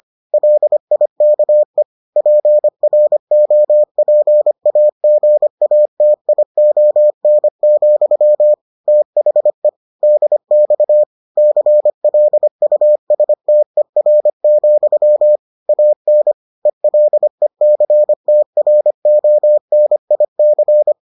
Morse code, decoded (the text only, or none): LIKE PROPAGATION, THE DX CLUSTER, AN ELECTRONIC